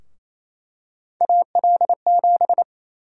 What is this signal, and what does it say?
Morse code, heard as a keyed tone, about 28 wpm, 700 Hz. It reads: AL7
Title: USA Call Sign Prefixes - 28wpm